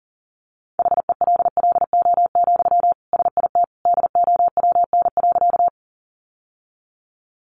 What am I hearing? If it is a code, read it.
Morse code, heard as a keyed tone, about 40 wpm, 700 Hz. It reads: HELLO, SIT DOWN.
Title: Sentences from Top 1100 Words 40wpm